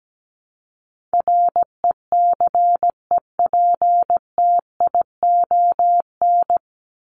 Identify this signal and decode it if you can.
Morse code, heard as a keyed tone, about 17 wpm, 700 Hz. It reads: RECEPTION
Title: Top 1751-2000 Words 17wpm